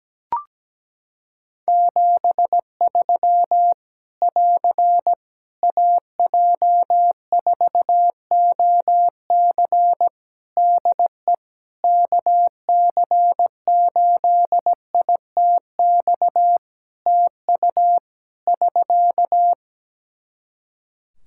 Morse code, heard as a keyed tone, about 17 wpm, 700 Hz. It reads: 73 <AR> AJ4OC DE KC8ITX TU <SK>
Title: Practice QSO - Exchange 4 - Ending - 17wpm